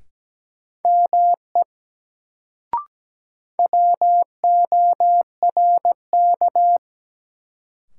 Morse code, heard as a keyed tone, about 17 wpm, 700 Hz. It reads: ME WORK
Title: Sentences from Top 200 Words - New Word Drill - 17wpm